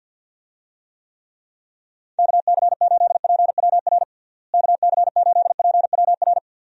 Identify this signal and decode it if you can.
Morse code, heard as a keyed tone, about 52 wpm, 700 Hz. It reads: KC8PWR KC8PWR
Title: USA Call Signs - 6 Characters - Repeated 3x - 50wpm